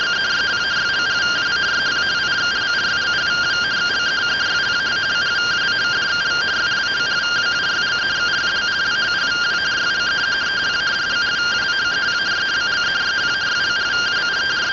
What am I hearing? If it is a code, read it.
Two-tone teleprinter signal, about 170 Hz shift.
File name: findings / 81 axial iodine 262.gen-35 rtty rx